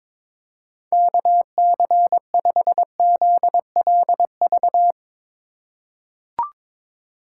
Morse code, heard as a keyed tone, about 21 wpm, 700 Hz. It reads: KC5ZLV
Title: USA Call Signs - 6 Characters - 4x Character Spacing - 22wpm